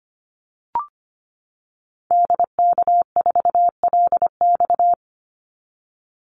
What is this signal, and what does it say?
Morse code, heard as a keyed tone, about 25 wpm, 700 Hz. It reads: DK4LX
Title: Call Signs - 25wpm